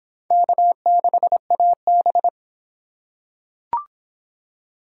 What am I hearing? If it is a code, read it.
Morse code, heard as a keyed tone, about 25 wpm, 700 Hz. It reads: K6AB